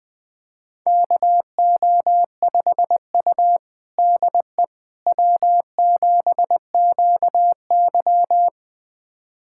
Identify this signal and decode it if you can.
Morse code, heard as a keyed tone, about 20 wpm, 700 Hz. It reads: KO5U DE W7QY
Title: USA Callsign DE Callsign - 4 Characters - 20wpm